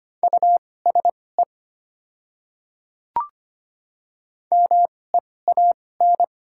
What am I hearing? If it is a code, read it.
Morse code, heard as a keyed tone, about 25 wpm, spaced as for 18 wpm, 700 Hz. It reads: USE MEAN
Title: 2x Character Spacing - Top 200 Words - 25wpm